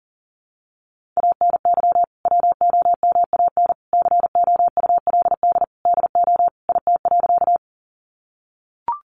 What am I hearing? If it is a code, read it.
Morse code, heard as a keyed tone, about 40 wpm, 700 Hz. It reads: ANY WOMAN COULD DO IT.
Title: Sentences from Top 1100 Words 40wpm